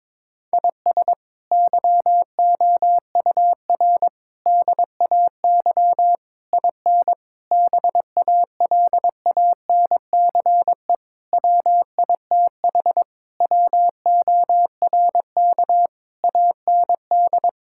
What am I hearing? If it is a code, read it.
Morse code, heard as a keyed tone, about 22 wpm, 700 Hz. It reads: IS YOUR DAY IN BALANCE WITH WORK AND
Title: Sentences from Top 1000 Words - Review Part 6 - 22wpm